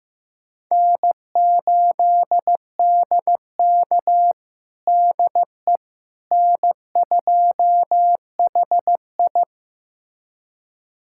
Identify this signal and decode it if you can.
Morse code, heard as a keyed tone, about 15 wpm, 700 Hz. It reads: N8DK DE N2HI
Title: USA Callsign DE Callsign - 4 Characters - 15wpm